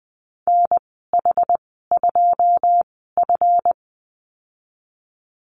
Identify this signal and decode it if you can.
Morse code, heard as a keyed tone, about 20 wpm, spaced as for 15 wpm, 700 Hz. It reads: NH2F
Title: 173 Lesson 28g - Number 9 - Call Signs 20wpm